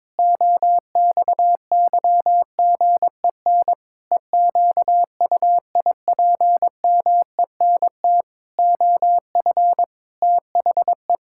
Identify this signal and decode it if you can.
Morse code, heard as a keyed tone, about 22 wpm, 700 Hz. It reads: OXYGEN EQUIPMENT OF THE
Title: Pangram Challenge v2 - 22wpm